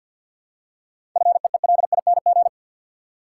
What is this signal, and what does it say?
Morse code, heard as a keyed tone, about 52 wpm, 700 Hz. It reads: FEELING
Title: Seven Letter Words - 50wpm